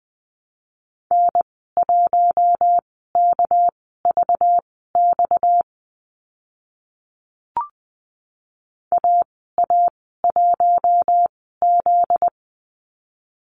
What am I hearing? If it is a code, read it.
Morse code, heard as a keyed tone, about 20 wpm, spaced as for 14 wpm, 700 Hz. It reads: N1KVX AA1Z
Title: 245 Lesson 39g - Letter z - Call Signs 20wpm